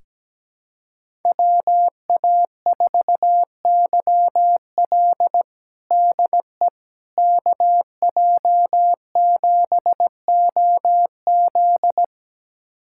Morse code, heard as a keyed tone, about 17 wpm, 700 Hz. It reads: WA4YL DE KJ7OZ
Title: USA Callsign DE Callsign 17wpm